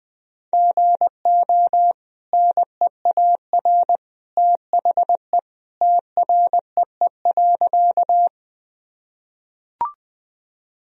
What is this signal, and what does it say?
Morse code, heard as a keyed tone, about 20 wpm, 700 Hz. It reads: GO NEAR THE TREE.
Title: Sentences from Top 1000 Words - Review Part 1 - 20wpm